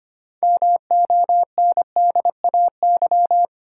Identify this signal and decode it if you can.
Morse code, heard as a keyed tone, about 25 wpm, 700 Hz. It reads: MONDAY
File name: practice / 25wpm / Easy Words with Hints - 25wpm